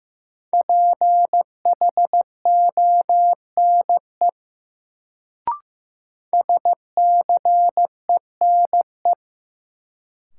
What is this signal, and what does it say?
Morse code, heard as a keyed tone, about 15 wpm, 700 Hz. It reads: PHONE SCENE
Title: Sentences from Top 1100 Words - New Word Drill - 15wpm